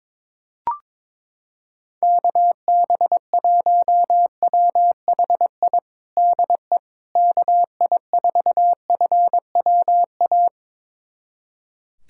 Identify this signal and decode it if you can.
Morse code, heard as a keyed tone, about 22 wpm, 700 Hz. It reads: KB1WHI DE KI4FWA
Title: USA Callsign DE Callsign - 6 Characters - 22wpm